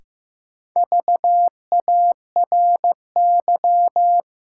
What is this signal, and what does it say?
Morse code, heard as a keyed tone, about 15 wpm, 700 Hz. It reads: VARY
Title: Sentences from Top 1200 Words - New Word Drill - 15wpm